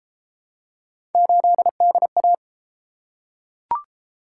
Morse code, heard as a keyed tone, about 30 wpm, 700 Hz. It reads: 8DA